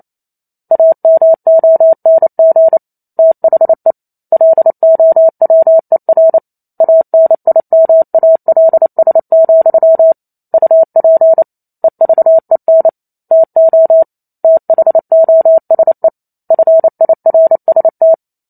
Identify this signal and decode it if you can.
Morse code, heard as a keyed tone, about 28 wpm, 650 Hz. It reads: AMONG THE LOWER ANIMALS, UP EVEN TO THOSE FIRST